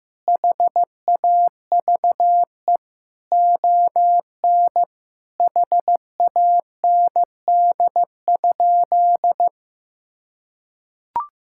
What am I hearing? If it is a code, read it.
Morse code, heard as a keyed tone, about 16 wpm, 700 Hz. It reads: HAVE ON HAND?
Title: Sentences from Top 1000 Words - Review Part 5 - 15wpm